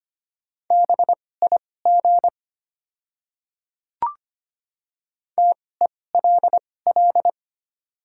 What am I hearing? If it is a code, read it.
Morse code, heard as a keyed tone, about 25 wpm, spaced as for 18 wpm, 690 Hz. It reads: BIG TELL
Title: 2x Character Spacing - Top 200 Words - 25wpm